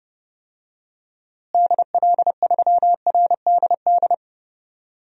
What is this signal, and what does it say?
Morse code, heard as a keyed tone, about 30 wpm, 700 Hz. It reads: DL3RDD
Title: DXCC-100 Call Signs - 6 Characters - Repeated 3x - 30wpm